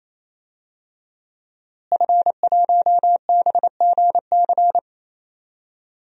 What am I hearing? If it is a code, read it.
Morse code, heard as a keyed tone, about 28 wpm, 700 Hz. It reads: F1BGC
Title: DXCC-100 Call Signs - 5 Characters - Repeated 3x - 28wpm